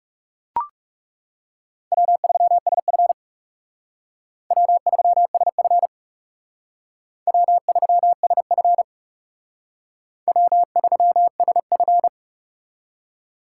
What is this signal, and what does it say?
Morse code, heard as a keyed tone, about 37 wpm, 700 Hz. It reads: W3SF W3SF W3SF W3SF
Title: Speed Racing - USA 1x2 Call Signs 30wpm